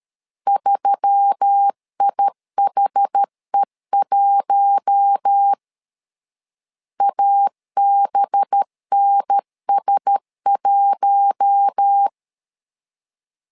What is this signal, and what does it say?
Morse code, heard as a keyed tone, about 13 wpm, 800 Hz. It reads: 3IHE1 ABNS1
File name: LEZIONI / 5 LBVC / 5 LB 12wpm 1906